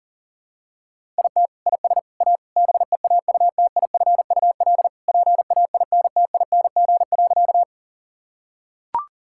Morse code, heard as a keyed tone, about 40 wpm, 700 Hz. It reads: IT IS A BEAUTIFUL PAINTING.
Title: Sentences from Top 1000 Words - Review Part 6 - 40wpm